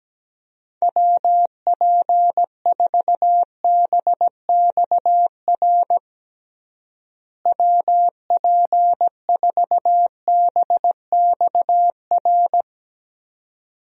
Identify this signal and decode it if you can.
Morse code, heard as a keyed tone, about 17 wpm, 700 Hz. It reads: WP4BXR WP4BXR
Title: USA Call Signs - 6 Characters - Repeated 2x - 17wpm